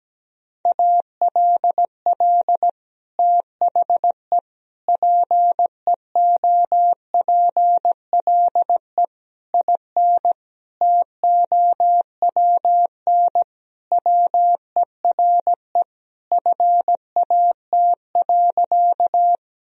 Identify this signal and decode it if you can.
Morse code, heard as a keyed tone, about 17 wpm, 700 Hz. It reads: ALL THE PEOPLE IN TOWN WERE FAT.
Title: Sentences from Top 1000 Words - Review Part 4 - 17wpm